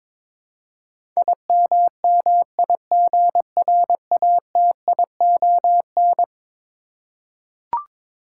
Morse code, heard as a keyed tone, about 22 wpm, 700 Hz. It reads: IMMIGRATION
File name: practice / 22wpm / Eleven Letter Words - 22wpm